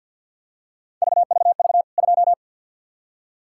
Morse code, heard as a keyed tone, about 48 wpm, 700 Hz. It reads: VVV 2
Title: K1USN SST QSO Practice 50wpm